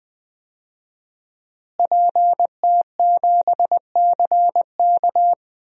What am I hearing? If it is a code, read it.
Morse code, heard as a keyed tone, about 20 wpm, 700 Hz. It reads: PT7CK
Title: DXCC-100 Call Signs - 5 Characters - Repeated 2x - 20wpm